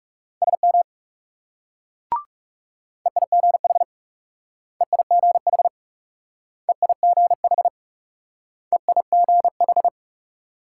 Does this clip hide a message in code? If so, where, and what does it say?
Yes: Morse code from 0.4 to 9.9 s, heard as a keyed tone, about 38 wpm, 700 Hz. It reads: SM EIGH EIGH EIGH EIGH